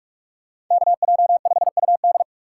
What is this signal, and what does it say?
Morse code, heard as a keyed tone, about 42 wpm, 700 Hz. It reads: KJ5UD